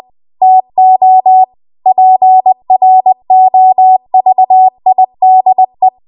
Morse code, heard as a keyed tone, about 20 wpm, 750 Hz. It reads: TO PROVIDE
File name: Archive / 20 WPM / 20WPM